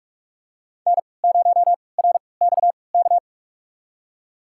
Morse code, heard as a keyed tone, about 42 wpm, spaced as for 25 wpm, 700 Hz. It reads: N0RXK
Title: USA Call Signs - 5 Characters - 3x Character Spacing - 45wpm